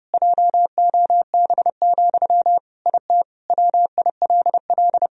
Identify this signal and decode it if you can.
Morse code, heard as a keyed tone, about 32 wpm, 700 Hz. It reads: JOB, IT WILL
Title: Sentences from Top 1000 Words - Review Part 4 - 30wpm